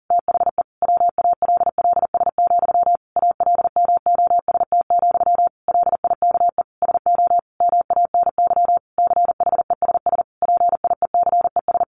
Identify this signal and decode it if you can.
Morse code, heard as a keyed tone, about 40 wpm, 700 Hz. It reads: THE WALLS, ALMOST, LIKE SO MANY CHESS PIECES
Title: A Sound Of Thunder 40wpm